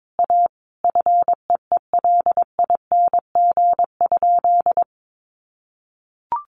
Morse code, heard as a keyed tone, about 22 wpm, 700 Hz. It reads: A FEELING?
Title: Sentences from Top 1200 Words 22wpm